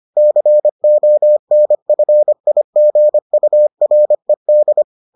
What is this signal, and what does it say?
Morse code, heard as a keyed tone, about 25 wpm, 600 Hz. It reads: CONFIGURED